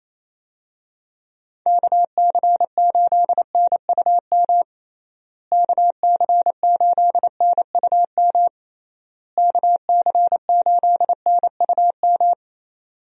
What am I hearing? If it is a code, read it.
Morse code, heard as a keyed tone, about 28 wpm, 700 Hz. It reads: KC8NUM KC8NUM KC8NUM
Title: USA Call Signs - 6 Characters - Repeated 3x - 28wpm